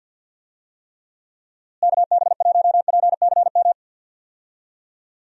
Morse code, heard as a keyed tone, about 50 wpm, 700 Hz. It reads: KB1PCM
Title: USA Call Signs - 6 Characters - Repeated 3x - 50wpm